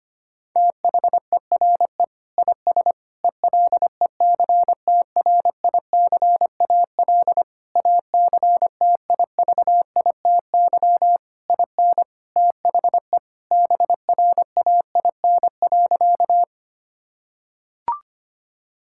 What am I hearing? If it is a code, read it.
Morse code, heard as a keyed tone, about 25 wpm, 700 Hz. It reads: THERE IS ELECTRICAL ACTIVITY IN THE BRAIN.
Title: Sentences from Top 1000 Words - Review Part 6 - 25wpm